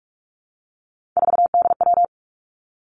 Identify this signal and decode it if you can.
Morse code, heard as a keyed tone, about 46 wpm, 700 Hz. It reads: 4DW
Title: Three Challenging Letter-Number Combinations 45wpm